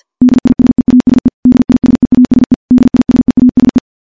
Morse code, heard as a keyed tone, about 51 wpm, 250 Hz. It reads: DISEASE DISEASE DISEASE
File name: morse / morse-training-50WPM